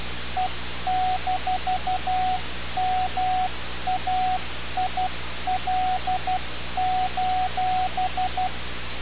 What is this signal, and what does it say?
Morse code, heard as a keyed tone, about 12 wpm, 740 Hz, with radio noise. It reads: E-MAIL:
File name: radiocap / REC z